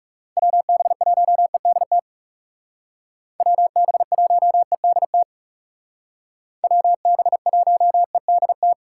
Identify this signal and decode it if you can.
Morse code, heard as a keyed tone, about 38 wpm, 700 Hz. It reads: WB1EDT WB1EDT WB1EDT